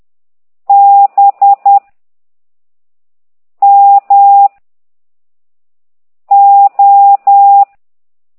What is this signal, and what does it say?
Morse code, heard as a keyed tone, about 10 wpm, 800 Hz. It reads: B M O
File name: ct2idw / exerc13